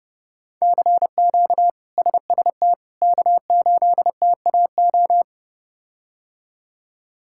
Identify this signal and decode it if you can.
Morse code, heard as a keyed tone, about 30 wpm, 700 Hz. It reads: CQ SST K8TAO
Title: K1USN SST QSO Practice - 30wpm